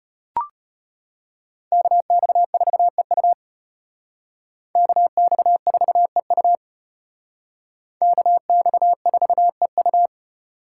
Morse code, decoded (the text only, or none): KX4EU KX4EU KX4EU